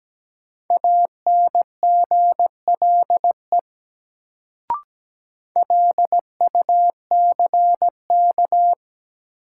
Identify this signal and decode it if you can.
Morse code, heard as a keyed tone, about 17 wpm, 700 Hz. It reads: ANGLE LUCK
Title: Sentences from Top 1500 Words - New Word Drill - 17wpm